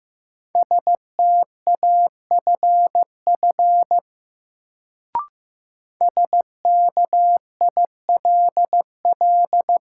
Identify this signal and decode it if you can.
Morse code, heard as a keyed tone, about 15 wpm, 700 Hz. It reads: STAFF SKILL
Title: Sentences from Top 1200 Words - New Word Drill - 15wpm